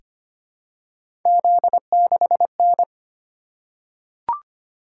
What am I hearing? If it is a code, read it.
Morse code, heard as a keyed tone, about 25 wpm, 700 Hz. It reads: Z6N
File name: Three Letter-Number Combinations - 25wpm